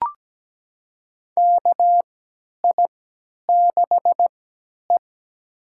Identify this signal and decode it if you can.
Morse code, heard as a keyed tone, about 17 wpm, spaced as for 10 wpm, 700 Hz. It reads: KI6E
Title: USA Call Signs - 4 Characters - 3x Character Spacing - 17wpm